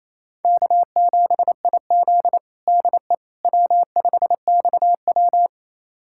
Morse code, heard as a keyed tone, about 28 wpm, 700 Hz. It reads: K7IZ DE W5XW
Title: USA Callsign DE Callsign - 4 Characters - 28wpm